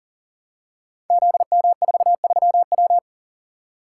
Morse code, heard as a keyed tone, about 40 wpm, 700 Hz. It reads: ZM43W
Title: Code Groups - 5 Letter:Numbers - 40wpm